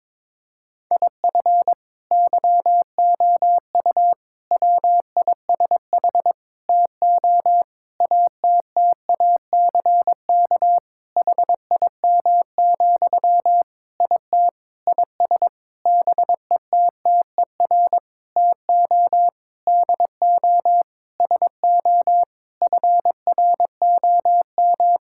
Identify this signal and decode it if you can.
Morse code, heard as a keyed tone, about 22 wpm, 700 Hz. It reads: IF YOU WISH TO ATTACK HIM, IT IS BETTER TO DO SO FROM
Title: Sentences from Top 1600 Words 22wpm